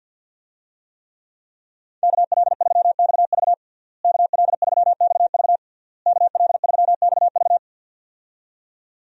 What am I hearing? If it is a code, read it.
Morse code, heard as a keyed tone, about 46 wpm, 700 Hz. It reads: KL3XV KL3XV KL3XV
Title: USA Call Signs - 5 Characters - Repeated 3x - 50wpm